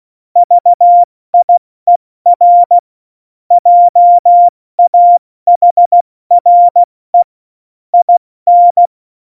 Morse code, heard as a keyed tone, about 16 wpm, 700 Hz. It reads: VIER JAHRE IN
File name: Morsekurs / Witze / Witz 18. Juni 2019, 70 F80